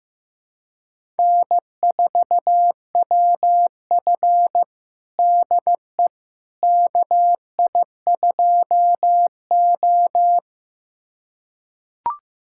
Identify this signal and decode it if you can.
Morse code, heard as a keyed tone, about 15 wpm, 700 Hz. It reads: N4WF DE KI2O
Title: USA Callsign DE Callsign - 4 Characters - 15wpm